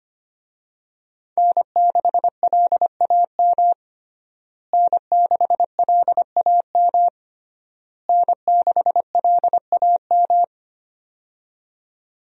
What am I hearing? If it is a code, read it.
Morse code, heard as a keyed tone, about 25 wpm, 700 Hz. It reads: N6LAM N6LAM N6LAM
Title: USA Call Signs - 5 Characters - Repeated 3x - 25wpm